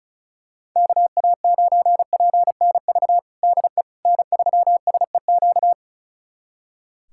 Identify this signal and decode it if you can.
Morse code, heard as a keyed tone, about 36 wpm, 700 Hz. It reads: KA9PNV DE N3SEQ
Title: USA Callsign DE Callsign 35wpm